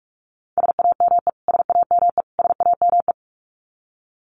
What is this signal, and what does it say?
Morse code, heard as a keyed tone, about 44 wpm, 700 Hz. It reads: SAME SAME SAME
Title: Top 500 Words Repeated 45wpm